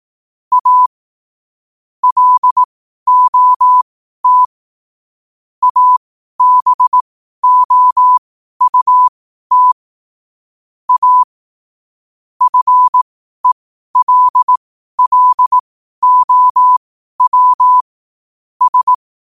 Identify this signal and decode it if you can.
Morse code, heard as a keyed tone, about 18 wpm, spaced as for 12 wpm, 1000 Hz. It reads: A LOT ABOUT A FELLOW S
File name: qotdfiles / 12wpm